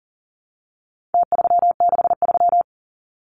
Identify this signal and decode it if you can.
Morse code, heard as a keyed tone, about 40 wpm, 700 Hz. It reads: T363
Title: Numbers - Contest Style - 4-Digit 40wpm